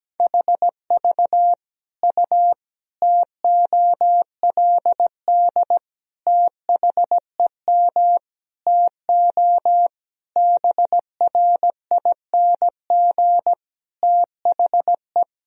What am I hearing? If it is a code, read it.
Morse code, heard as a keyed tone, about 17 wpm, 700 Hz. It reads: HV U TOLD THEM TO BRING THE
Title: Sentences from Top 500 Words and Abbreviations 17wpm